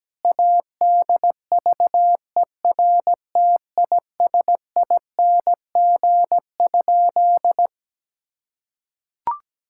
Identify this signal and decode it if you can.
Morse code, heard as a keyed tone, about 17 wpm, 700 Hz. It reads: ADVERTISING?
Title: Sentences from Top 1000 Words - Review Part 6 - 17wpm